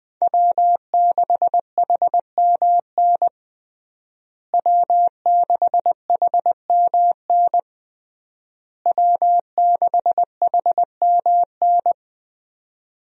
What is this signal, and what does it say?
Morse code, heard as a keyed tone, about 21 wpm, 700 Hz. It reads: W6HMN W6HMN W6HMN
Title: USA Call Signs - 5 Characters - Repeated 3x - 20wpm